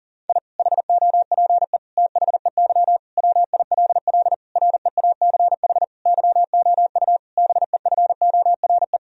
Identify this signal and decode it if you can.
Morse code, heard as a keyed tone, about 40 wpm, 700 Hz. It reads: I HOPE THEY WILL REACH YOU BEFORE
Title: Sentences from Top 1000 Words - Review Part 2 - 40wpm